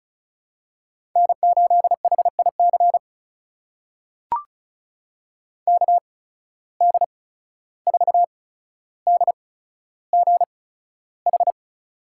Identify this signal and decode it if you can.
Morse code, heard as a keyed tone, about 36 wpm, spaced as for 14 wpm, 700 Hz. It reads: N8HIC KD4DGH
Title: USA Call Signs - 8x Character Spacing - 35wpm